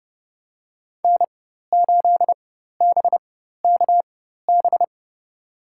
Morse code, heard as a keyed tone, about 32 wpm, spaced as for 14 wpm, 700 Hz. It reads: N8BKB
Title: USA Call Signs - 5 Characters - 4x Character Spacing - 30wpm